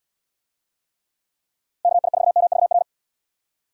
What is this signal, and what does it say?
Morse code, heard as a keyed tone, about 71 wpm, 700 Hz. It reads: KI4MVR